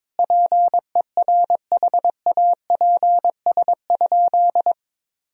Morse code, heard as a keyed tone, about 21 wpm, 700 Hz. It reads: PERHAPS?
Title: Sentences from Top 1000 Words - Review Part 2 - 22wpm